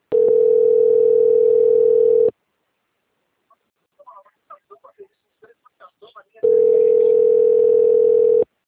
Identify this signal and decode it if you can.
Ringback tone (North America), 440 + 480 Hz, with speech.
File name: rg-1600-unknown- - - .3577